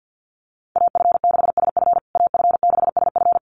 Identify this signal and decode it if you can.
Morse code, heard as a keyed tone, about 50 wpm, 700 Hz. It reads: AF6SF AF6SF
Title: USA Call Signs Repeated 3x 50wpm